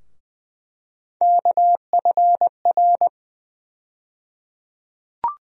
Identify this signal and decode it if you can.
Morse code, heard as a keyed tone, about 21 wpm, 700 Hz. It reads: KFR
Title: USA Call Sign Postfixes - 3 Letters 20wpm